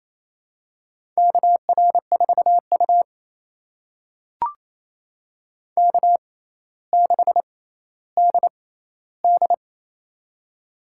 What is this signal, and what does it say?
Morse code, heard as a keyed tone, about 28 wpm, spaced as for 14 wpm, 700 Hz. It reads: KR4U K6DD
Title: USA Call Signs - 4 Characters - 6x Character Spacing - 28wpm